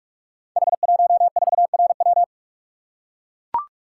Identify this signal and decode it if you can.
Morse code, heard as a keyed tone, about 44 wpm, 700 Hz. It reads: H14RW
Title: Code Groups - 5 Letter:Numbers - 45wpm